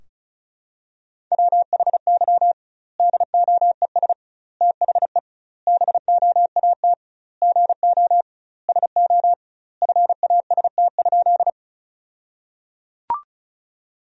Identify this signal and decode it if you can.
Morse code, heard as a keyed tone, about 36 wpm, 700 Hz. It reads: WHY DOES THE BOAT GO SO FAST?
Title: 2x Word-Spacing - Sentences from Top 500 Words - 35wpm